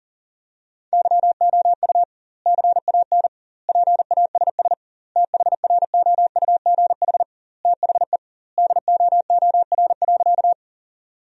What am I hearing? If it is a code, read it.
Morse code, heard as a keyed tone, about 40 wpm, 700 Hz. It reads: YOU CAN PASS THROUGH THE DOOR.
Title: 2x Word-Spacing - Sentences from Top 400 Words - 40wpm